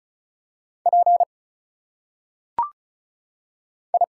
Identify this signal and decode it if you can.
Morse code, heard as a keyed tone, about 35 wpm, 700 Hz. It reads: P I
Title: Single Letters - Phonetic Pronunciation 35wpm